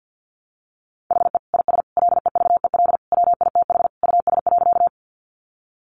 Morse code, heard as a keyed tone, about 50 wpm, 700 Hz. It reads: HE IS LEVEL WITH US.
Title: Sentences from Top 700 Words 50wpm